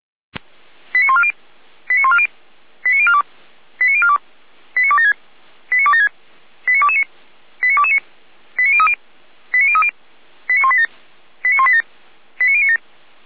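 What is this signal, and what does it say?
Five-tone selective call (ZVEI), heard as quick runs of five tones, tones of about 70 ms. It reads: 89140 89140 89042 89042 89276 89276 89229 89229 89030 89030 89187 89187 89097
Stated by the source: ZVEI five-tone selective call received by SDR.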